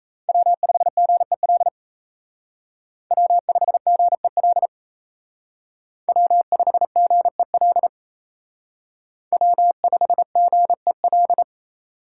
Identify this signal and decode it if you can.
Morse code, heard as a keyed tone, about 33 wpm, 700 Hz. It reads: W5GEL W5GEL W5GEL W5GEL
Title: Speed Racing - USA 1x3 Call Signs 28wpm